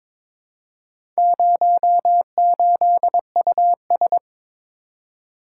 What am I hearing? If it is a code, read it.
Morse code, heard as a keyed tone, about 21 wpm, 700 Hz. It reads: 08US
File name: practice / 22wpm / Four Letter-Number Combinations - 22wpm